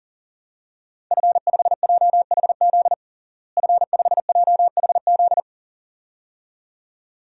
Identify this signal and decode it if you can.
Morse code, heard as a keyed tone, about 40 wpm, 700 Hz. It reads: F5JHZ F5JHZ
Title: DXCC-100 Call Signs - 5 Characters - Repeated 2x - 40wpm